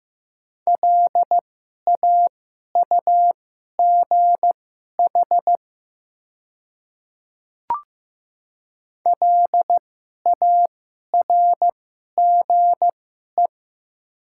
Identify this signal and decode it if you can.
Morse code, heard as a keyed tone, about 16 wpm, spaced as for 11 wpm, 700 Hz. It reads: LAUGH LARGE
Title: 2x Character Spacing - Top 500 Words - 15wpm